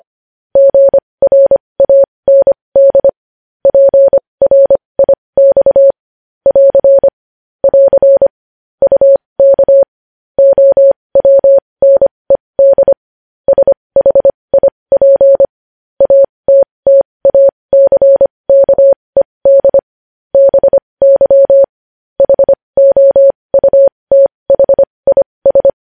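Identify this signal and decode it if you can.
Morse code, heard as a keyed tone, about 25 wpm, spaced as for 20 wpm, 550 Hz. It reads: GRAND PRIX <AR> <AR> UK OWNED SHIP ATTACKED BY HOUTHIS
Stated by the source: News headlines sent in Morse code at 20.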